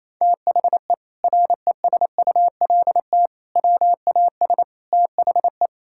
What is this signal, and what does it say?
Morse code, heard as a keyed tone, about 28 wpm, 700 Hz. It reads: THE RESULT WAS THE
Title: Sentences from Top 1000 Words - Review Part 5 - 28wpm